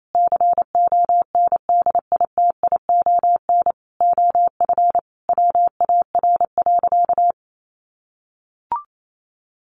Morse code, heard as a keyed tone, about 28 wpm, 700 Hz. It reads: CONDITION OF WAR.
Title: Sentences from Top 1500 Words 28wpm